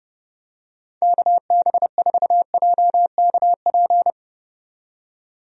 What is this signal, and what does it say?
Morse code, heard as a keyed tone, about 30 wpm, 700 Hz. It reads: KB4JKP
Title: USA Call Signs - 6 Characters - 30wpm